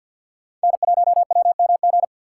Morse code, heard as a keyed tone, about 47 wpm, 700 Hz. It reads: N1WMG
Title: USA Call Signs - 5 Characters - 4x Character Spacing - 50wpm